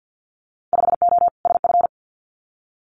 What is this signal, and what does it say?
Morse code, heard as a keyed tone, about 50 wpm, 700 Hz. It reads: 5O SF
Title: Winter Field Day Exchanges Encore 50wpm